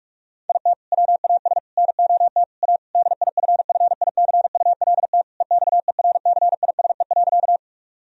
Morse code, heard as a keyed tone, about 41 wpm, 700 Hz. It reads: IT WAS NOT A DIFFICULT EXERCISE.
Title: Sentences from Top 1000 Words - Review Part 4 - 45wpm